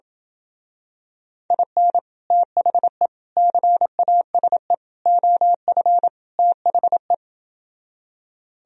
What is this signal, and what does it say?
Morse code, heard as a keyed tone, about 28 wpm, 700 Hz. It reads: IN THE CASE OF THE